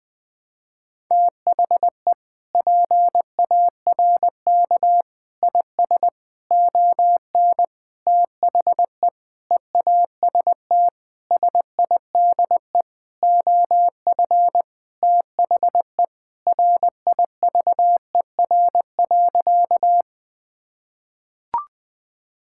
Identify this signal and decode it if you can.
Morse code, heard as a keyed tone, about 21 wpm, 700 Hz. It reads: THE PARK IS ON THE EAST SIDE OF THE RIVER.
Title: Sentences with 6-Word Phrases - Spoken Beginning - 20wpm